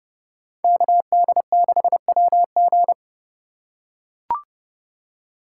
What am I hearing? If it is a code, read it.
Morse code, heard as a keyed tone, about 30 wpm, 700 Hz. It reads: KD6WG